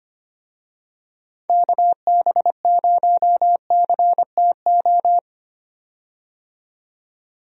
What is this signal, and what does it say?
Morse code, heard as a keyed tone, about 25 wpm, 700 Hz. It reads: KB0CTO
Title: USA Call Signs - 6 Characters - 25wpm